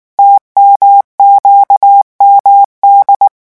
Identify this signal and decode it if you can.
Morse code, heard as a keyed tone, about 19 wpm, 800 Hz. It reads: TMQMD
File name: morse mp3 / revision7-vitesse 18 mots